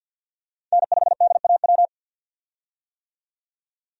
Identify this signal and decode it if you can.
Morse code, heard as a keyed tone, about 52 wpm, 700 Hz. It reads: N5DAW
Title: USA Call Signs - 5 Characters - 50wpm